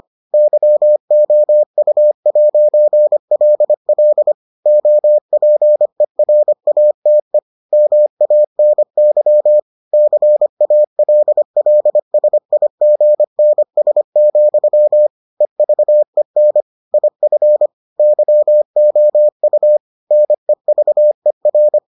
Morse code, heard as a keyed tone, about 25 wpm, 600 Hz. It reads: YOU'LL OPERATE MANY CALLSIGNS, EVEN IF YOU NEVER